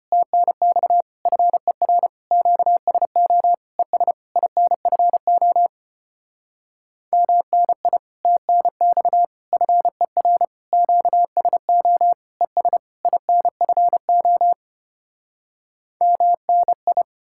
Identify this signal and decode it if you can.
Morse code, heard as a keyed tone, about 33 wpm, 700 Hz. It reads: TNX FER QSO ES INFO MNI TNX FER QSO ES INFO MNI